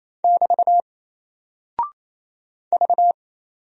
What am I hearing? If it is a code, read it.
Morse code, heard as a keyed tone, about 28 wpm, 700 Hz. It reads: <BT> V
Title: Single Character (Letter:Number:Punctuation:Prosign) Alt 28wpm